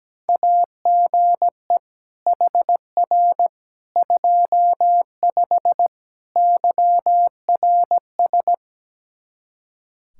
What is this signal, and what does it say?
Morse code, heard as a keyed tone, about 17 wpm, 700 Hz. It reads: AGE HR 25 YRS
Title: K1USN SST QSO Practice 17wpm